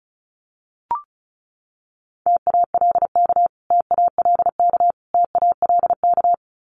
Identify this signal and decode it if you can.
Morse code, heard as a keyed tone, about 34 wpm, 700 Hz. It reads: TALK TALK TALK
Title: Top 500 Words Repeated 35wpm